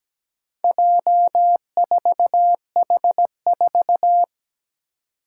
Morse code, heard as a keyed tone, about 17 wpm, 700 Hz. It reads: J4H4
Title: Four Challenging Letter-Number Combinations 17wpm